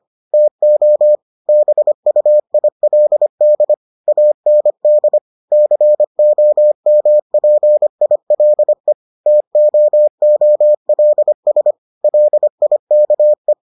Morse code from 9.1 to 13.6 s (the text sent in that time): TOOLS LIKE